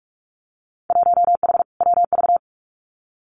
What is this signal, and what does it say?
Morse code, heard as a keyed tone, about 46 wpm, 700 Hz. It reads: 1H WV
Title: Winter Field Day Exchanges 45wpm